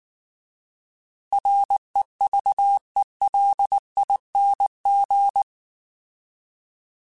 Morse code, heard as a keyed tone, about 19 wpm, 800 Hz. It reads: REVELING
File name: Alphabetical / Alphabetical mp3s / Rs @18